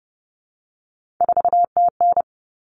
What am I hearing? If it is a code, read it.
Morse code, heard as a keyed tone, about 30 wpm, 700 Hz. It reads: 4TN
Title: Common Cut Numbers - 3-Digit 30wpm